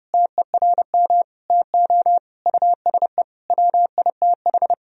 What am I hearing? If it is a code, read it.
Morse code, heard as a keyed tone, about 30 wpm, 700 Hz. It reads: TERM TO USE WITH
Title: Sentences from Top 1000 Words - Review Part 4 - 30wpm